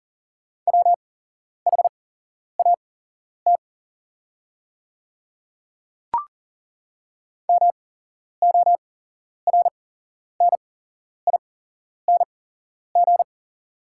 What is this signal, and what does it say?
Morse code, heard as a keyed tone, about 40 wpm, spaced as for 11 wpm, 690 Hz. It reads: WHAT MORNING
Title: 8x Character Spacing - Top 500 Words - 40wpm